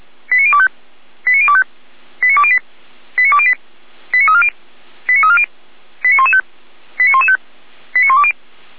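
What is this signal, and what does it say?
Five-tone selective call (ZVEI), heard as quick runs of five tones, tones of about 70 ms. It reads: 89025 89025 89208 89208 89340 89340 89115 89115 89120
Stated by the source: ZVEI five-tone call 89270 received by SDR, from feuerwehr-orlamuende.org fire brigade audio server.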